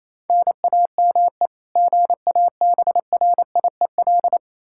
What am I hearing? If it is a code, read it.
Morse code, heard as a keyed tone, about 28 wpm, 700 Hz. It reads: NAME GABRIEL
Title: K1USN SST QSO Practice 28wpm